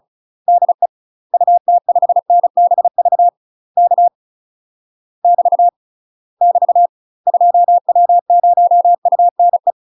Morse code, heard as a keyed tone, about 34 wpm, 700 Hz. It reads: DE UT5NBV K <BT> <BT> 2W0UNE
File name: UK MASTER QSOs TRAINING / 35WPM / UK MASTER QSOs 0008